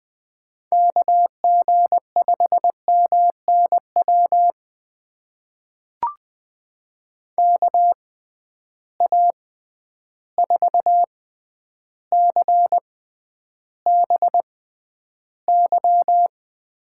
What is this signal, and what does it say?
Morse code, heard as a keyed tone, about 20 wpm, spaced as for 10 wpm, 700 Hz. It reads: KG5MNW KA4CBY
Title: USA Call Signs - 6 Characters - 6x Character Spacing - 20wpm